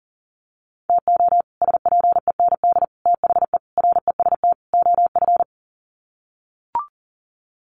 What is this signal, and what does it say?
Morse code, heard as a keyed tone, about 40 wpm, 700 Hz. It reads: TO SPEND THE REST OF
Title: Sets of 5 Words Encore2 40wpm